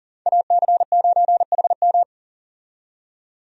Morse code, heard as a keyed tone, about 38 wpm, 700 Hz. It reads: AC9HM
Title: USA Call Signs - 5 Characters - 40wpm